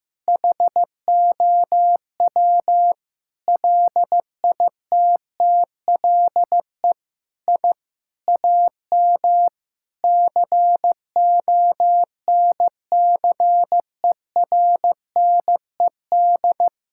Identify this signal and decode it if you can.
Morse code, heard as a keyed tone, about 15 wpm, 700 Hz. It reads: HOW LITTLE I AM CONCERNED
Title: Sentences from Top 1000 Words - Review Part 6 - 15wpm